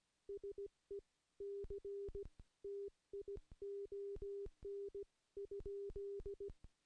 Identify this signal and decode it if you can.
Morse code, heard as a keyed tone, about 17 wpm, spaced as for 15 wpm, 400 Hz. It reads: SECTION?
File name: contests / CW-SS-k5tr- -0700